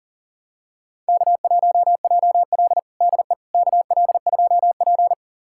Morse code, heard as a keyed tone, about 40 wpm, 700 Hz. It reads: K1JL DE KL2P
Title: USA Callsign DE Callsign - 4 Characters - 40wpm